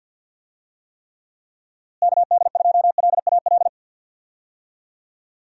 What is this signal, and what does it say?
Morse code, heard as a keyed tone, about 52 wpm, 700 Hz. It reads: KD2LAL
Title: USA Call Signs - 6 Characters - Repeated 2x - 50wpm